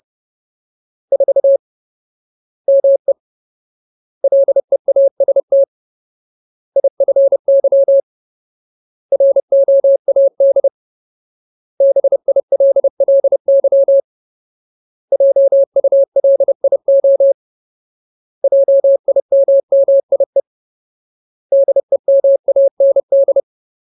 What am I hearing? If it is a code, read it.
Morse code, heard as a keyed tone, about 30 wpm, 550 Hz. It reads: 4 ME LEAST IFY ROAD BILLY JULIO JIMMIE DEMAND
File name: Ultimate Word List / 30WPM Ultimate